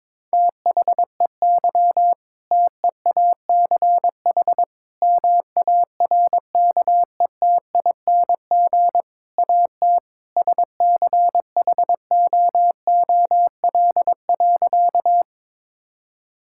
Morse code, heard as a keyed tone, about 22 wpm, 700 Hz. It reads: THEY TEACH MARKETING AT SCHOOL.
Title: Sentences from Top 1000 Words - Review Part 5 - 22wpm